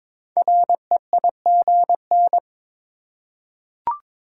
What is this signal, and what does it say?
Morse code, heard as a keyed tone, about 22 wpm, 700 Hz. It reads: REIGN
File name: 5 Letter Words - First 3 Letters Given - 22wpm